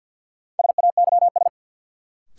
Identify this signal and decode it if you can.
Morse code, heard as a keyed tone, about 48 wpm, 700 Hz. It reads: SAYS
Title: Sets of 2 Words - New Word Drill - 50wpm